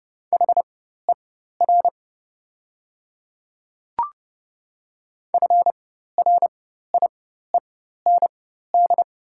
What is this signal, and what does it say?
Morse code, heard as a keyed tone, about 30 wpm, spaced as for 14 wpm, 690 Hz. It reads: HER FRIEND
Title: 4x Character Spacing - Top 300 Words - 30wpm